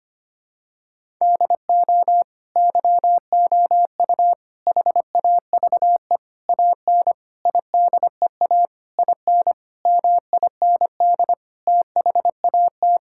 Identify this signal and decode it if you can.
Morse code, heard as a keyed tone, about 25 wpm, 700 Hz. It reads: DO YOU HAVE AN IDEA IN MIND THAT
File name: Sets of 9 Words - 25wpm